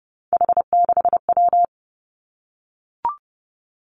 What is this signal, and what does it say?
Morse code, heard as a keyed tone, about 30 wpm, 700 Hz. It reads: H6W
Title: Three Challenging Letter-Number Combinations 30wpm